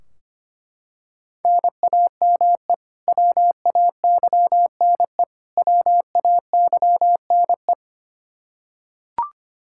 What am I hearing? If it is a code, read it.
Morse code, heard as a keyed tone, about 25 wpm, 700 Hz. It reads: NAME WAYNE WAYNE
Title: Practice QSO - Exchange 1 - Beginning - 25wpm